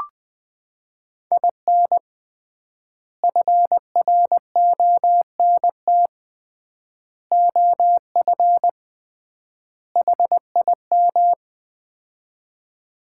Morse code, heard as a keyed tone, about 20 wpm, 700 Hz. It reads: IN FRONT OF HIM
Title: 3x Word-Spacing - Sets of 4 Words - 20wpm